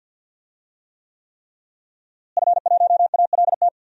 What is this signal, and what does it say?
Morse code, heard as a keyed tone, about 52 wpm, 700 Hz. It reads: F1ALT
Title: DXCC-100 Call Signs - 5 Characters - Repeated 3x - 50wpm